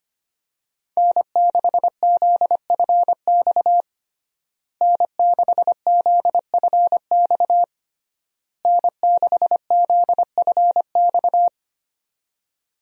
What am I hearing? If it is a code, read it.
Morse code, heard as a keyed tone, about 25 wpm, 700 Hz. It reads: N6ZFX N6ZFX N6ZFX
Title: USA Call Signs - 5 Characters - Repeated 3x - 25wpm